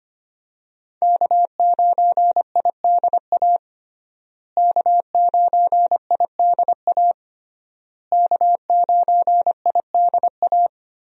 Morse code, heard as a keyed tone, about 25 wpm, 700 Hz. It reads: K9IDA K9IDA K9IDA
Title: USA Call Signs - 5 Characters - Repeated 3x - 25wpm